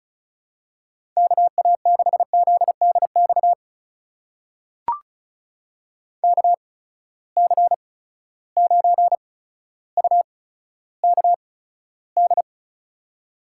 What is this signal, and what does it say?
Morse code, heard as a keyed tone, about 36 wpm, spaced as for 15 wpm, 700 Hz. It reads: KA6ZDX KC9UKD
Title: USA Call Signs - 6 Characters - 8x Character Spacing - 35wpm